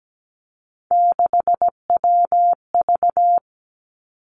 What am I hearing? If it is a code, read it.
Morse code, heard as a keyed tone, about 17 wpm, 700 Hz. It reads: 6WV
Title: Three Challenging Letter-Number Combinations 17wpm